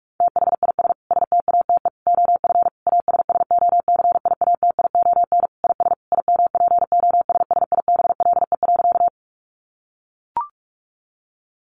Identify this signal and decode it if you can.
Morse code, heard as a keyed tone, about 44 wpm, 700 Hz. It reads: THIS STATE OF ASSOCIATION IS IMPOSSIBLE.
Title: Sentences from Top 1100 Words 45wpm